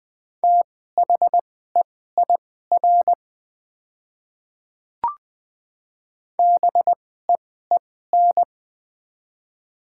Morse code, heard as a keyed tone, about 20 wpm, spaced as for 15 wpm, 690 Hz. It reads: THEIR BEEN
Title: 2x Character Spacing - Top 100 Words - 20wpm